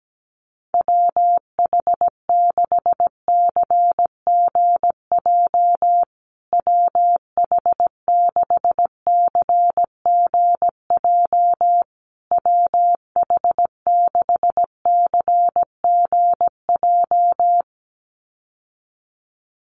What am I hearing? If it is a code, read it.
Morse code, heard as a keyed tone, about 17 wpm, 700 Hz. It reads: WH6CGJ WH6CGJ WH6CGJ
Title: USA Call Signs Repeated 3x 17wpm